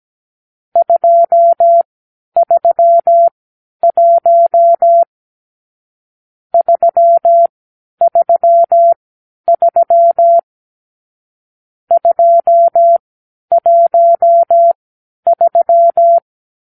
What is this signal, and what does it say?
Morse code, heard as a keyed tone, about 17 wpm, spaced as for 10 wpm, 680 Hz. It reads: 231 333 213
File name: k7qo / Lesson 44 (1-3)